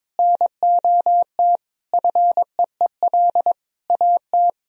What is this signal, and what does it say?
Morse code, heard as a keyed tone, about 21 wpm, 700 Hz. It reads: NOT FEEL AT
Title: Sentences from Top 1000 Words - Review Part 1 - 22wpm